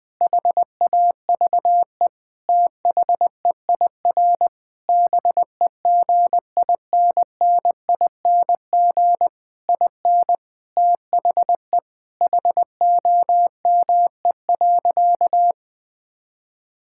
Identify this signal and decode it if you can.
Morse code, heard as a keyed tone, about 21 wpm, 700 Hz. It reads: HAVE THEIR BEGINNING IN THE HOME.
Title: Sentences from Top 1000 Words - Review Part 6 - 20wpm